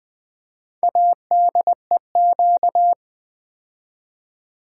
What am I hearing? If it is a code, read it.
Morse code, heard as a keyed tone, about 20 wpm, 700 Hz. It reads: ADEQ
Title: Build A Word - 2 Letters At A Time - 8-Letter Words - 20wpm